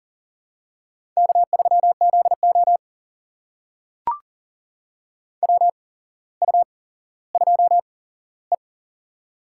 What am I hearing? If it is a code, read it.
Morse code, heard as a keyed tone, about 40 wpm, spaced as for 17 wpm, 700 Hz. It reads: K3ZO WU2E
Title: USA Call Signs - 4 Characters - 8x Character Spacing - 40wpm